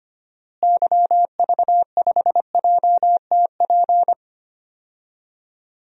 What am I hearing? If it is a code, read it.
Morse code, heard as a keyed tone, about 25 wpm, 700 Hz. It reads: YV5JTP
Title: DXCC-100 Call Signs - 6 Characters - Repeated 3x - 25wpm